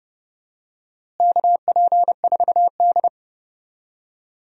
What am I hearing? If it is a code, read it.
Morse code, heard as a keyed tone, about 30 wpm, 700 Hz. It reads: KP4D
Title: DXCC-100 Call Signs - 4 Characters - Repeated 2x - 30wpm